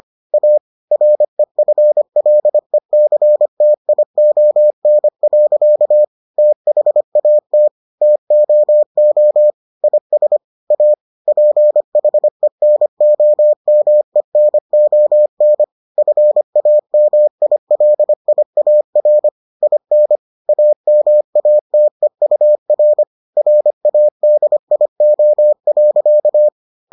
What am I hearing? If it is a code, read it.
Morse code, heard as a keyed tone, about 25 wpm, 600 Hz. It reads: A REFLECTION. THAT TOO IS A PHENOMENON FAMILIAR IN AMATEUR RADIO.